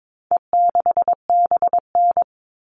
Morse code, heard as a keyed tone, about 22 wpm, 700 Hz. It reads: E6BN
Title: All Cut Numbers - 4-Digit 22wpm